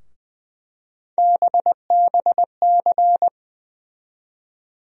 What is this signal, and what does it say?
Morse code, heard as a keyed tone, about 20 wpm, 700 Hz. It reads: BBC
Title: USA Call Sign Postfixes - 3 Letters 20wpm